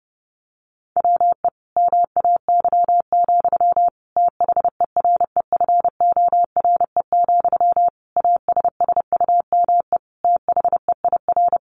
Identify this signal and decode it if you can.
Morse code, heard as a keyed tone, about 30 wpm, 700 Hz. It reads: WE MAY, THEREFORE, ASSUME THEIR
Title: Sentences from Top 1500 Words 30wpm